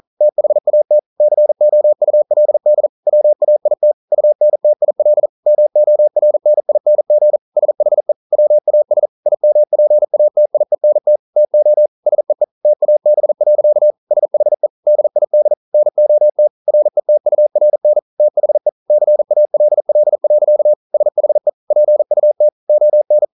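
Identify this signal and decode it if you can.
Morse code, heard as a keyed tone, about 42 wpm, 600 Hz. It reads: THAT COULD WAIT UNTIL MORNING SHE WAS IMPATIENT TO SEE TAB. SHE DID NOT RETURN THE CALL. SHE PUT ON
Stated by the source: Morse code practice at 41 wpm, from CW med Gnister.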